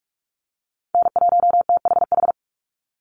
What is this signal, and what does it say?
Morse code, heard as a keyed tone, about 44 wpm, 700 Hz. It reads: N1THH
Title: USA Call Signs 45wpm